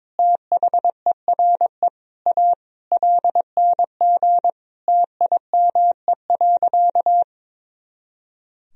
Morse code, heard as a keyed tone, about 22 wpm, 700 Hz. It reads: THERE A LNG TIME.
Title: Sentences from Top 400 Words and Abbreviations 22wpm